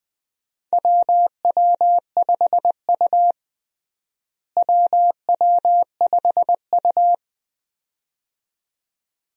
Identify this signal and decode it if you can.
Morse code, heard as a keyed tone, about 21 wpm, 700 Hz. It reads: WW5U WW5U
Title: USA Call Signs - 4 Characters - Repeated 2x - 20wpm